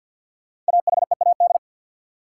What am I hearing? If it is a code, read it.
Morse code, heard as a keyed tone, about 48 wpm, 700 Hz. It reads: AHEAD
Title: Sentences from Top 1100 Words - New Word Drill Alt - 50wpm